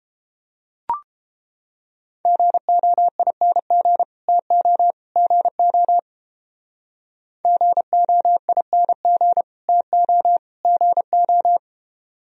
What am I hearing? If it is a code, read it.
Morse code, heard as a keyed tone, about 30 wpm, 700 Hz. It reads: GOING TO GO GOING TO GO